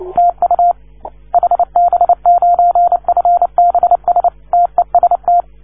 Morse code, heard as a keyed tone, about 29 wpm, 690 Hz. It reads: TU HB9FBS TEST
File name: 1esempio-telegrafia